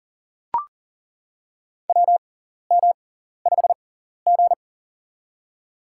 Morse code, heard as a keyed tone, about 40 wpm, spaced as for 14 wpm, 700 Hz. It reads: WM5G
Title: USA Call Signs - 4 Characters - 6x Character Spacing - 40wpm